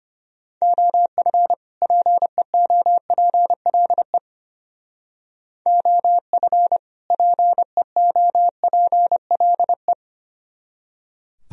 Morse code, heard as a keyed tone, about 32 wpm, 700 Hz. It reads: OF PEOPLE OF PEOPLE